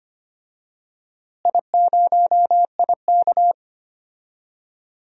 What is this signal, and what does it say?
Morse code, heard as a keyed tone, about 25 wpm, 700 Hz. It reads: I0IK
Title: DXCC-100 Call Signs - 4 Characters - Repeated 2x - 25wpm